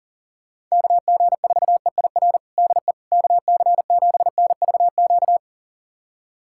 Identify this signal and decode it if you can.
Morse code, heard as a keyed tone, about 40 wpm, 700 Hz. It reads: KG4EIR DE KC7NVQ
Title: USA Callsign DE Callsign 40wpm